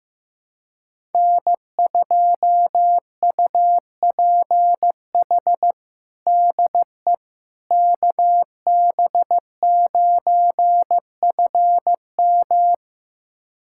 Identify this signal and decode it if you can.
Morse code, heard as a keyed tone, about 15 wpm, 700 Hz. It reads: N2UPH DE KB9FM
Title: USA Callsign DE Callsign - 5 Characters - 15wpm